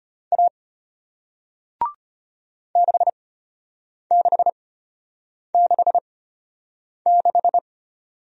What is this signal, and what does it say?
Morse code, heard as a keyed tone, about 33 wpm, spaced as for 8 wpm, 690 Hz. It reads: A 6666